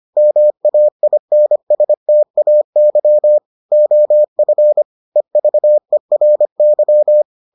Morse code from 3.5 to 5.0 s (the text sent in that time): OF